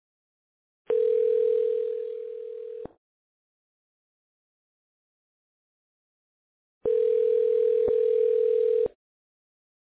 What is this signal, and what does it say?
Ringback tone (North America), 440 + 480 Hz.